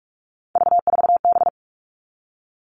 Morse code, heard as a keyed tone, about 44 wpm, 700 Hz. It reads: V4B
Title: All Cut Numbers - 3-Digit 45wpm